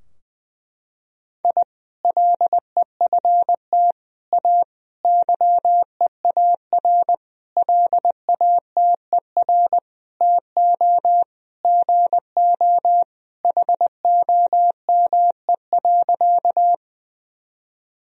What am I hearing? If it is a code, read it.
Morse code, heard as a keyed tone, about 20 wpm, 700 Hz. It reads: I LEFT A YEAR LATER TO GO HOME.
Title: Sentences from Top 1000 Words - Review Part 4 - 20wpm